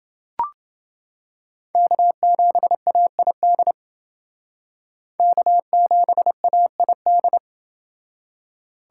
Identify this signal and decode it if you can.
Morse code, heard as a keyed tone, about 29 wpm, 700 Hz. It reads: K7AID K7AID